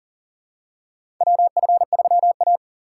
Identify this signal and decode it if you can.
Morse code, heard as a keyed tone, about 40 wpm, 700 Hz. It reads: WF3A
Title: USA Call Signs - 4x Character Spacing - 40wpm